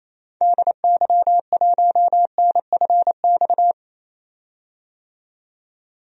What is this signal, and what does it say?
Morse code, heard as a keyed tone, about 28 wpm, 700 Hz. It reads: DY1NFX
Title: DXCC-100 Call Signs - 6 Characters - 28wpm